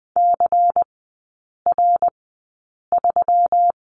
Morse code, heard as a keyed tone, about 19 wpm, spaced as for 8 wpm, 700 Hz. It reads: CR3
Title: 166 Lesson 27e - Number 7 - Three Characters 20wpm